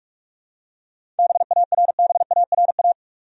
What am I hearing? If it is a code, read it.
Morse code, heard as a keyed tone, about 46 wpm, 700 Hz. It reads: BARBARA
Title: USA Common Names - 45wpm